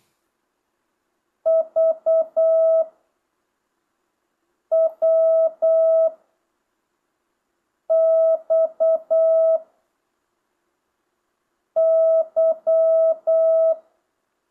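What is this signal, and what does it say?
Morse code, heard as a keyed tone, about 8 wpm, 660 Hz. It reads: V W X Y
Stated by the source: Morse lesson from JH8CBH's Zoom Morse course, session 5: letters V–Z, random 5-character and 10-character groups with 4-second gaps.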